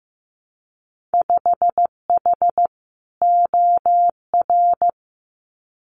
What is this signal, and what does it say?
Morse code, heard as a keyed tone, about 15 wpm, 700 Hz. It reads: 5H OR
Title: Winter Field Day Exchanges Encore 15wpm